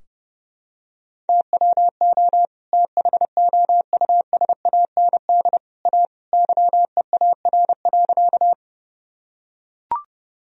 Morse code, heard as a keyed tone, about 32 wpm, 700 Hz. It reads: TWO THOUSAND A YEAR.
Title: Sentences from Top 1000 Words - Review Part 3 - 30wpm